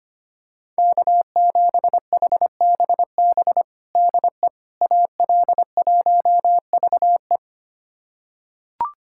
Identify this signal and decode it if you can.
Morse code, heard as a keyed tone, about 25 wpm, 700 Hz. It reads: K7HBB DE AL1VE
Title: USA Callsign DE Callsign - 5 Characters - 25wpm